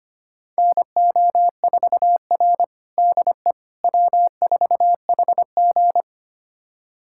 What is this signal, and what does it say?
Morse code, heard as a keyed tone, about 25 wpm, 700 Hz. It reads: NO4R DE W4HG
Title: USA Callsign DE Callsign - 4 Characters - 25wpm